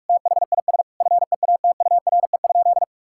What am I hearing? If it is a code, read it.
Morse code, heard as a keyed tone, about 46 wpm, 700 Hz. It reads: THIS FEATURE?
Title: Sentences from Top 1000 Words - Review Part 5 - 45wpm